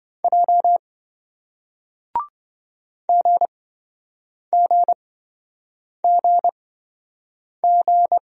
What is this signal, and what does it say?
Morse code, heard as a keyed tone, about 27 wpm, spaced as for 7 wpm, 700 Hz. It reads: J GGGG